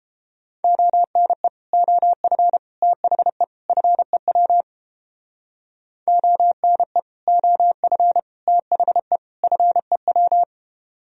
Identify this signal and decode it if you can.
Morse code, heard as a keyed tone, about 31 wpm, 700 Hz. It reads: ONE OF THE FEW ONE OF THE FEW